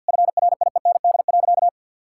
Morse code, heard as a keyed tone, about 50 wpm, 700 Hz. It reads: FRIEND.
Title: Sentences from Top 1000 Words - Review Part 5 - 50wpm